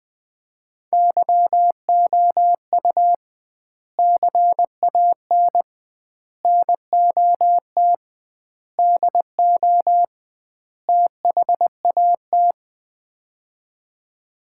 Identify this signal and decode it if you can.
Morse code, heard as a keyed tone, about 21 wpm, 700 Hz. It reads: YOU CAN NOT DO THAT
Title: 2x Word-Spacing - Sets of 5 Words - 20wpm